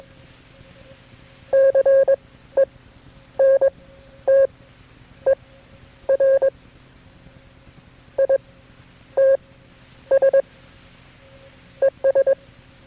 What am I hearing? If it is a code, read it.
Morse code, heard as a keyed tone, about 22 wpm, spaced as for 10 wpm, 540 Hz. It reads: CENTER ITS ES
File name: KB6MT-tape3-side1